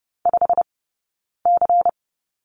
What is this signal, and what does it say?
Morse code, heard as a keyed tone, about 30 wpm, 700 Hz. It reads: 5 C
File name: 092 Lesson 16d - Letter c - Two Characters 30wpm